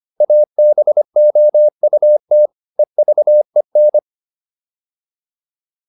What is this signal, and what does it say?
Morse code, heard as a keyed tone, about 25 wpm, 600 Hz. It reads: ABOUT EVEN
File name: PR405 25